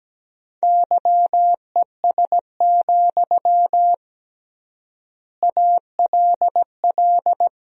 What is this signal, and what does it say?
Morse code, heard as a keyed tone, about 17 wpm, 700 Hz. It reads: YES, ALL
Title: 3x Word-Spacing - Sentences from Top 500 Words - 17wpm